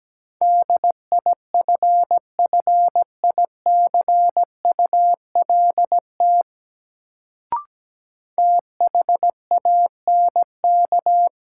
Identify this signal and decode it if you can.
Morse code, heard as a keyed tone, about 17 wpm, 700 Hz. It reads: DIFFICULT THANK
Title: Sentences from Top 600 Words - New Word Drill - 17wpm